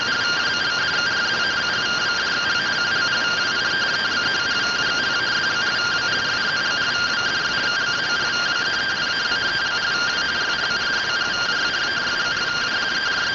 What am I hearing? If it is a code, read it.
Two-tone teleprinter signal, about 170 Hz shift.